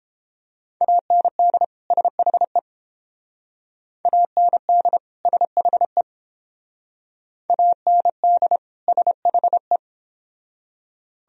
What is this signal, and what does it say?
Morse code, heard as a keyed tone, about 28 wpm, 700 Hz. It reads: AND SHE AND SHE AND SHE